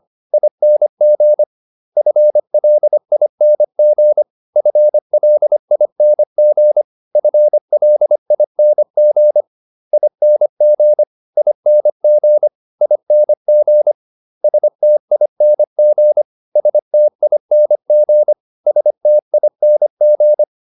Morse code, heard as a keyed tone, about 25 wpm, 600 Hz. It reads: ING FLING FLING FLING ING ING ING STING STING STING